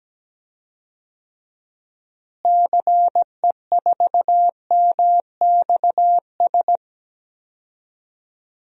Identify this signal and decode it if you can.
Morse code, heard as a keyed tone, about 17 wpm, 700 Hz. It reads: CE4MXS
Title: DXCC-100 Call Signs - 6 Characters - Repeated 3x - 17wpm